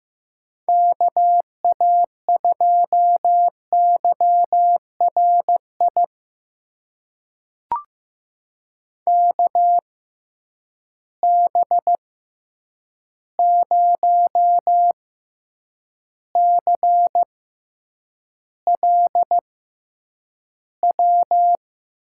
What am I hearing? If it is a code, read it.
Morse code, heard as a keyed tone, about 15 wpm, spaced as for 8 wpm, 700 Hz. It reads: KA2YRI KB0CLW
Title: USA Call Signs - 6 Characters - 6x Character Spacing - 15wpm